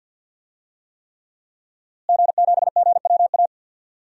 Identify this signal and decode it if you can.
Morse code, heard as a keyed tone, about 52 wpm, 700 Hz. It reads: G7GWA
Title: DXCC-100 Call Signs - 5 Characters - Repeated 3x - 50wpm